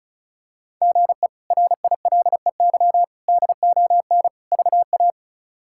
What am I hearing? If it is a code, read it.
Morse code, heard as a keyed tone, about 36 wpm, 700 Hz. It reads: GE RILEY DON VA
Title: K1USN SST QSO Practice - 35wpm